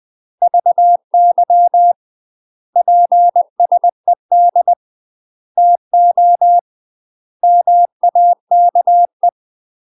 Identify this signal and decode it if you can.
Morse code, heard as a keyed tone, about 20 wpm, 700 Hz. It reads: VY PSED TO MAKE
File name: UK MASTER QSOs TRAINING / 20WPM / UK MASTER QSOs 0035